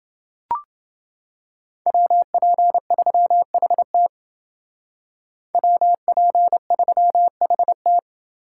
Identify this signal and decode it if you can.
Morse code, heard as a keyed tone, about 31 wpm, 700 Hz. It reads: WP3HT WP3HT